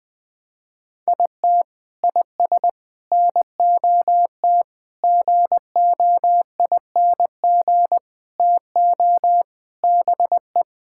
Morse code, heard as a keyed tone, about 21 wpm, 700 Hz. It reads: IT IS NOT GOING TO BE
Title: Sentences with 5-Word Phrases - Spoken Beginning - 20wpm